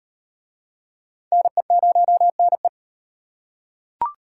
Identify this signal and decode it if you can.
Morse code, heard as a keyed tone, about 38 wpm, 700 Hz. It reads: NE0NE